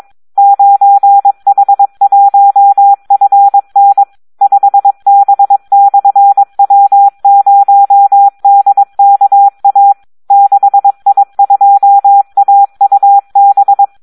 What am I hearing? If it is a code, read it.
Morse code, heard as a keyed tone, about 22 wpm, 800 Hz. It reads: 9H1FN 5B/W0DKA 6I2AUB
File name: Audio practica CW / SD 1 22